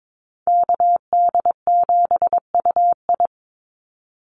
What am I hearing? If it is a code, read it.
Morse code, heard as a keyed tone, about 22 wpm, 700 Hz. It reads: KD7UI
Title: Call Signs Repeated 2x - 22wpm